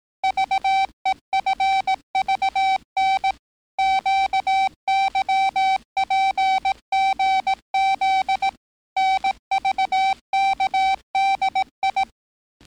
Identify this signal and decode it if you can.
Morse code, heard as a keyed tone, about 18 wpm, 770 Hz. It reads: VEFVN QYPGZ NVKDI
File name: training / NAVPERS 11049to11060 / NAVPERS 11058 RA RB